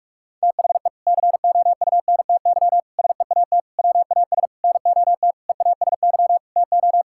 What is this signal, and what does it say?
Morse code, heard as a keyed tone, about 45 wpm, 700 Hz. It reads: THE COUNTY SEAT WAS NOT EASY TO
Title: Sentences from Top 1600 Words 45wpm